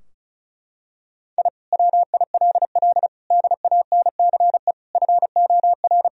Morse code, heard as a keyed tone, about 35 wpm, 700 Hz. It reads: I WILL DANCE FOR
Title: Sentences from Top 1000 Words - Review Part 4 - 35wpm